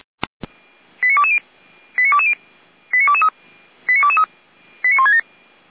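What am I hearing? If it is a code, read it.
Five-tone selective call (ZVEI), heard as quick runs of five tones, tones of about 70 ms. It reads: 89220 89223 89167
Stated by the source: Five-tone ZVEI selective call received on SDR.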